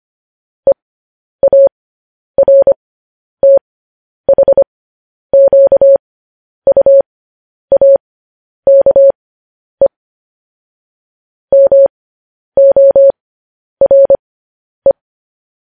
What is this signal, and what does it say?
Morse code, heard as a keyed tone, about 25 wpm, spaced as for 10 wpm, 550 Hz. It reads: EARTHQUAKE MORE
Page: news-headlines-in-morse-code-at-10-wpm